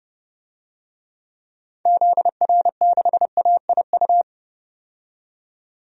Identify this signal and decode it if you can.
Morse code, heard as a keyed tone, about 30 wpm, 700 Hz. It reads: ZR6AIU
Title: DXCC-100 Call Signs - 6 Characters - 30wpm